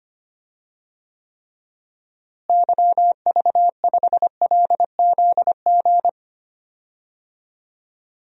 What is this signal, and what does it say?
Morse code, heard as a keyed tone, about 25 wpm, 700 Hz. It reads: YV5LZG
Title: DXCC-100 Call Signs - 6 Characters - Repeated 3x - 25wpm